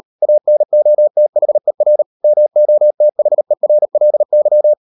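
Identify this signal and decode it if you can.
Morse code, heard as a keyed tone, about 38 wpm, 600 Hz. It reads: ANOTHER MOTHERLY
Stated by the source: Morse code practice at 38 WPM, sending Ken Follett's novel Never in English, from CW med Gnister.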